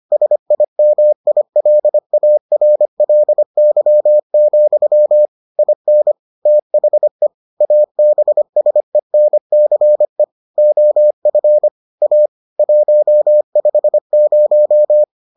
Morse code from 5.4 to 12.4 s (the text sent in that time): IN THE ABSENCE OF A